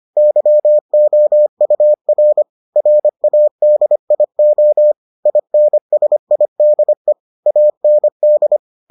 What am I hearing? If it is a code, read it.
Morse code, heard as a keyed tone, about 25 wpm, 600 Hz. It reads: YOUR RADIO INSIDE AND